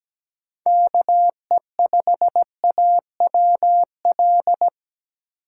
Morse code, heard as a keyed tone, about 17 wpm, 700 Hz. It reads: KE5AWL
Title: USA Call Signs - 4x Character Spacing - 17wpm